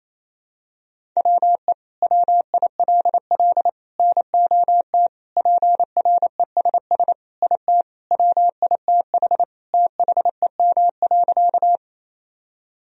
Morse code, heard as a keyed tone, about 28 wpm, 700 Hz. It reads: WE WILL NOT PRESS IT WITH THEM.
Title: Sentences from Top 1000 Words - Review Part 1 - 28wpm